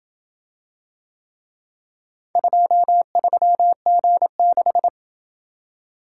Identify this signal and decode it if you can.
Morse code, heard as a keyed tone, about 27 wpm, 700 Hz. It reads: 23G6